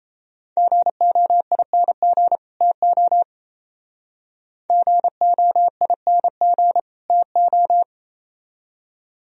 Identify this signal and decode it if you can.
Morse code, heard as a keyed tone, about 33 wpm, 700 Hz. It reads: GOING TO GOING TO
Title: Speed-Racing - Sets of 2 Words 28wpm